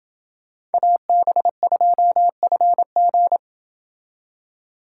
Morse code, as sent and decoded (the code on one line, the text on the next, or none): .- -... ..--- ..-. --.
AB2FG